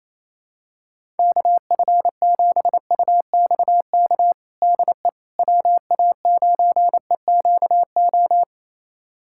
Morse code, heard as a keyed tone, about 28 wpm, 700 Hz. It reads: KF7UXK DE WA9EQO
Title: USA Callsign DE Callsign 28wpm